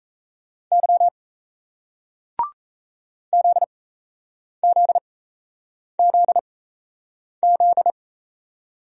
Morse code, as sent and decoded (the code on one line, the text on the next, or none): -.-- / --.. --.. --.. --..
Y ZZZZ